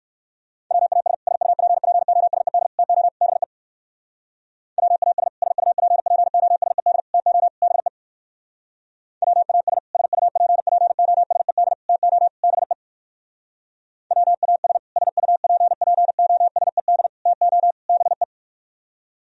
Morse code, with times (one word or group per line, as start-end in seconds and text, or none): 0.7-1.1 WAS
1.3-2.7 SUPPOSED
2.8-3.1 TO
3.2-3.4 BE
4.8-5.3 WAS
5.4-7.0 SUPPOSED
7.1-7.5 TO
7.6-7.9 BE
9.2-9.8 WAS
9.9-11.7 SUPPOSED
11.9-12.3 TO
12.4-12.7 BE
14.1-14.8 WAS
15.0-17.1 SUPPOSED
17.3-17.7 TO
17.9-18.2 BE